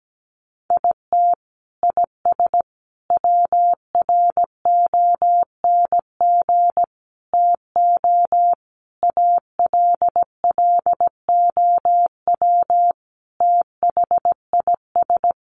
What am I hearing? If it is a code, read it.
Morse code, heard as a keyed tone, about 17 wpm, 700 Hz. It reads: IT IS WRONG TO ALLOW THIS
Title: Sentences from Top 1300 Words 17wpm